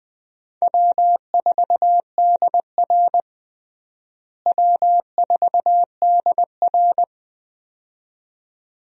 Morse code, heard as a keyed tone, about 20 wpm, 700 Hz. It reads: W4DR W4DR
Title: USA Call Signs - 4 Characters - Repeated 2x - 20wpm